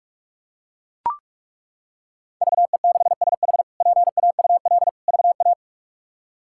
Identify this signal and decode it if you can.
Morse code, heard as a keyed tone, about 41 wpm, 700 Hz. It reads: VE6SH PAUL VA
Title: K1USN SST QSO Practice - 45wpm